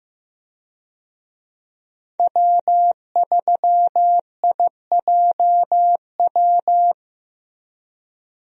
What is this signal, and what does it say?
Morse code, heard as a keyed tone, about 15 wpm, 700 Hz. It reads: W3IJW
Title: USA Call Signs - 5 Characters - 15wpm